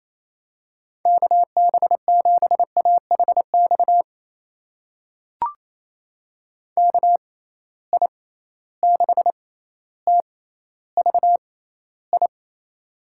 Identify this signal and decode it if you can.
Morse code, heard as a keyed tone, about 28 wpm, spaced as for 14 wpm, 700 Hz. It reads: KB7AHX KI6TVI
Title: USA Call Signs - 6 Characters - 6x Character Spacing - 28wpm